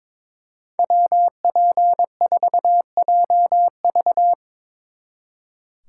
Morse code, heard as a keyed tone, about 25 wpm, 700 Hz. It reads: WP4JV
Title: Speed Racing - USA Call Signs 22wpm